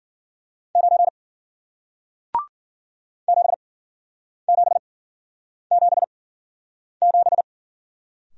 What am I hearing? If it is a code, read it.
Morse code, heard as a keyed tone, about 54 wpm, spaced as for 9 wpm, 700 Hz. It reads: 9 7777